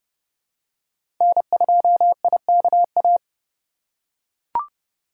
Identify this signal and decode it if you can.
Morse code, heard as a keyed tone, about 31 wpm, 700 Hz. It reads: N2IKA
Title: USA Call Signs - 5 Characters - 4x Character Spacing - 30wpm